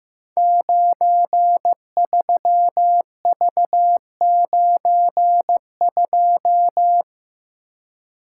Morse code, heard as a keyed tone, about 15 wpm, 700 Hz. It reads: 93V92
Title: Code Groups - 5 Letter:Numbers - 15wpm